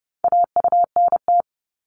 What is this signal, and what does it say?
Morse code, heard as a keyed tone, about 30 wpm, 700 Hz. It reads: AUNT
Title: Multi-letter Phonemes 30wpm